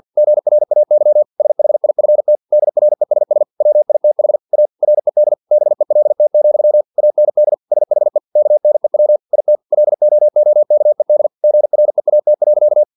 Morse code, heard as a keyed tone, about 50 wpm, 600 Hz. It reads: GRAY SHIFT DRESS WITH A RED BELT, AND SHE KNEW IT LOOKED GREAT.
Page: Never 49wpm